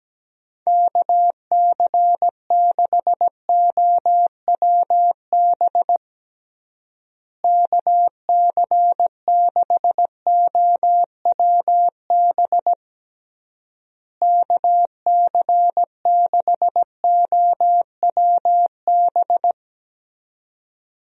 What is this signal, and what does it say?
Morse code, heard as a keyed tone, about 17 wpm, 700 Hz. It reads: KC6OWB KC6OWB KC6OWB
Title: USA Call Signs - 6 Characters - Repeated 3x - 17wpm